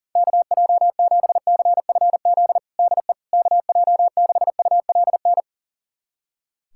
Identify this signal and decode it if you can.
Morse code, heard as a keyed tone, about 37 wpm, 700 Hz. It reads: KJ7CFZ DE KJ6ULN
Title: USA Callsign DE Callsign - 6 Characters - 40wpm